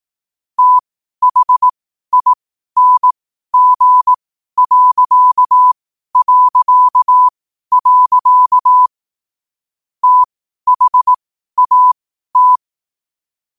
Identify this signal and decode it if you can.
Morse code, heard as a keyed tone, about 18 wpm, spaced as for 12 wpm, 1000 Hz. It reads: THING... THAT
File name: qotdfiles / 12wpm